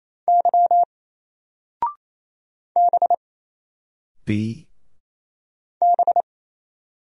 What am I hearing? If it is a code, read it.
Morse code, heard as a keyed tone, about 27 wpm, 700 Hz. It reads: Y B B
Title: Single Letter-Number 28wpm